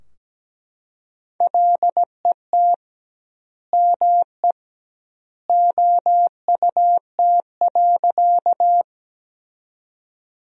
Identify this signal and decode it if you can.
Morse code, heard as a keyed tone, about 17 wpm, 700 Hz. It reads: LET ME OUT.
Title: 2x Word-Spacing - Sentences from Top 300 Words - 17wpm